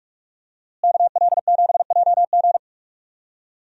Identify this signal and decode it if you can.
Morse code, heard as a keyed tone, about 46 wpm, 700 Hz. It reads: KL7JG
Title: DXCC-100 Call Signs - 5 Characters - Repeated 2x - 45wpm